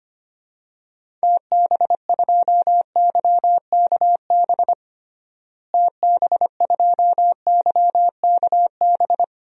Morse code, heard as a keyed tone, about 25 wpm, 700 Hz. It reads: TB2YKB TB2YKB
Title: DXCC-100 Call Signs - 6 Characters - Repeated 2x - 25wpm